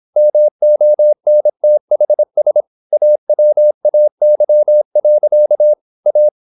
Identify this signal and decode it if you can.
Morse code, heard as a keyed tone, about 26 wpm, 600 Hz. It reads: MONTHS AWAY. A
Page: Never 26wpm | CW med Gnister